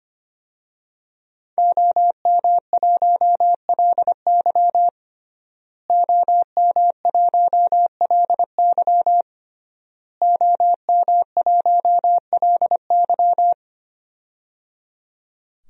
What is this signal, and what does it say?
Morse code, heard as a keyed tone, about 25 wpm, 700 Hz. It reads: OM1LY OM1LY OM1LY
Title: DXCC-100 Call Signs - 5 Characters - Repeated 3x - 25wpm